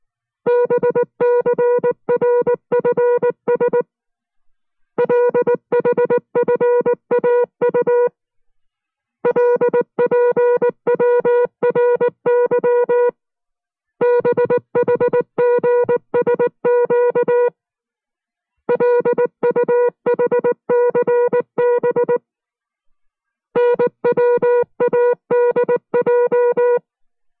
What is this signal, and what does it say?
Morse code, heard as a keyed tone, about 19 wpm, 480 Hz. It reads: BCRFS LHFAU LPWRY BHGSQ LUHCB NWADJ